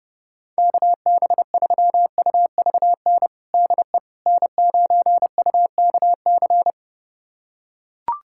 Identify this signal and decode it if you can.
Morse code, heard as a keyed tone, about 32 wpm, 700 Hz. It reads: KB3UVN DE N9UKC
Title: USA Callsign DE Callsign 30wpm